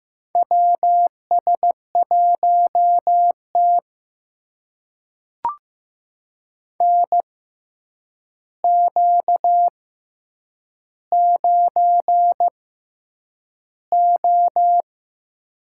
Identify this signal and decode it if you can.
Morse code, heard as a keyed tone, about 15 wpm, spaced as for 8 wpm, 700 Hz. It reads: WS1T NQ9O
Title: USA Call Signs - 4 Characters - 6x Character Spacing - 15wpm